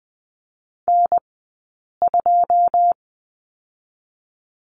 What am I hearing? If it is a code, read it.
Morse code, heard as a keyed tone, about 20 wpm, 700 Hz. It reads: N 2
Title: 086 Lesson 15d - Number 5 - Two Characters 20wpm